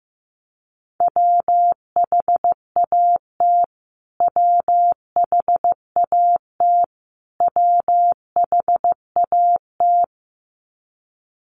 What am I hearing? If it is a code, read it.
Morse code, heard as a keyed tone, about 15 wpm, 700 Hz. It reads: WHAT WHAT WHAT
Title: Top 200 Words Repeated 15wpm